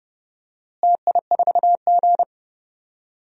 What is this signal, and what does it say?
Morse code, heard as a keyed tone, about 29 wpm, 700 Hz. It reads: TI4G
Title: DXCC-100 Call Signs - 4 Characters - Repeated 3x - 30wpm